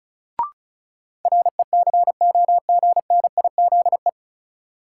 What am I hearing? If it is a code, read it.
Morse code, heard as a keyed tone, about 34 wpm, 700 Hz. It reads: RECOGNIZE
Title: Sentences from Top 1400 Words - New Word Drill - 35wpm